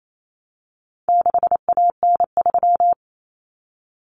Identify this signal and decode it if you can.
Morse code, heard as a keyed tone, about 28 wpm, 700 Hz. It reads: 6AN3
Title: Common Cut Numbers - 4-Digit 28wpm